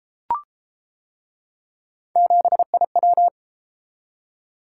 Morse code, heard as a keyed tone, about 31 wpm, 700 Hz. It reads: 7IW